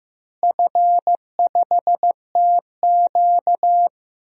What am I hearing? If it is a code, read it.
Morse code, heard as a keyed tone, about 15 wpm, 700 Hz. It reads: F5TQ
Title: DXCC-100 Call Signs - 4 Characters - Repeated 2x - 15wpm